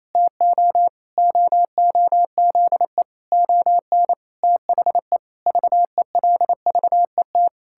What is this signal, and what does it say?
Morse code, heard as a keyed tone, about 28 wpm, 700 Hz. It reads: TO OOZE ON THE VELVET
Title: Pangram Challenge v2 - 28wpm